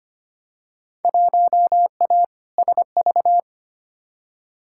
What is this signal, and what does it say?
Morse code, heard as a keyed tone, about 25 wpm, 700 Hz. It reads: 1A SV
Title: Field Day Exchanges - 25wpm